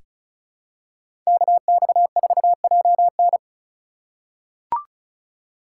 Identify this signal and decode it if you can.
Morse code, heard as a keyed tone, about 34 wpm, 700 Hz. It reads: KX4JN
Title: USA Call Signs - 5 Characters - 2x Character Spacing - 35wpm